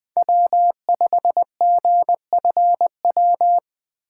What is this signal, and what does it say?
Morse code, heard as a keyed tone, about 20 wpm, 700 Hz. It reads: W5GFW
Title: USA Call Signs - 5 Characters - 2x Character Spacing - 20wpm